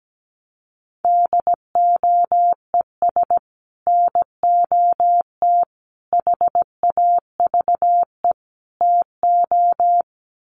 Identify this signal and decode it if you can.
Morse code, heard as a keyed tone, about 17 wpm, 700 Hz. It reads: DOES NOT HAVE TO
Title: Sets of 4 Words 17wpm